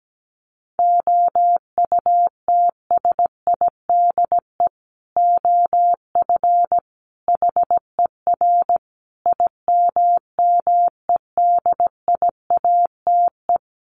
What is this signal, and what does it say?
Morse code, heard as a keyed tone, about 17 wpm, 700 Hz. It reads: OUTSIDE OF HER IMMEDIATE
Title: Sentences from Top 1200 Words 17wpm